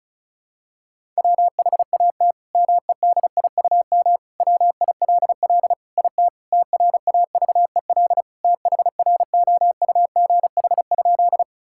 Morse code, heard as a keyed tone, about 36 wpm, 700 Hz. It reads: WHAT MEDIUM WILL IT TRAVEL THROUGH?
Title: Sentences from Top 1000 Words - Review Part 6 - 35wpm